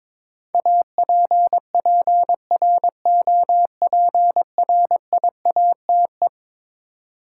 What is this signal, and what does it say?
Morse code, heard as a keyed tone, about 21 wpm, 700 Hz. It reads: APPROPRIATE
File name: practice / 22wpm / Eleven Letter Words - 22wpm